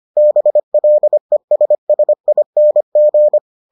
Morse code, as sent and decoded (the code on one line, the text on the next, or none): -... .-.. ... ... .. -. --.
BLESSING